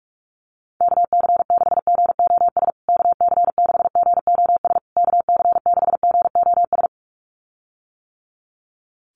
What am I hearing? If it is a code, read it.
Morse code, heard as a keyed tone, about 44 wpm, 700 Hz. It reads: KC6GOS KC6GOS KC6GOS
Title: USA Call Signs Repeated 3x 45wpm